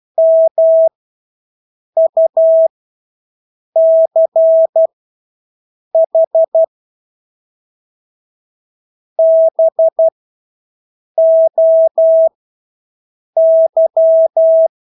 Morse code, heard as a keyed tone, about 12 wpm, spaced as for 6 wpm, 650 Hz. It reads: MUCH BOY
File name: Engelske ord 12-6 wpm